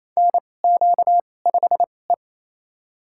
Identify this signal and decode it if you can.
Morse code, heard as a keyed tone, about 28 wpm, spaced as for 20 wpm, 700 Hz. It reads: NQ5E
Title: USA Call Signs - 4 Characters - 2x Character Spacing - 28wpm